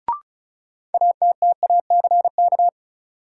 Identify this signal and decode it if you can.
Morse code, heard as a keyed tone, about 35 wpm, 700 Hz. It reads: ATTACK
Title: Sentences from Top 1500 Words - New Word Drill - 35wpm